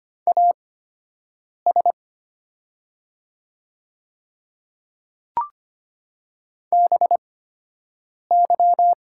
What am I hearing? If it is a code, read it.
Morse code, heard as a keyed tone, about 25 wpm, spaced as for 7 wpm, 700 Hz. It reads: AS BY
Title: 8x Character Spacing - Top 100 Words - 25wpm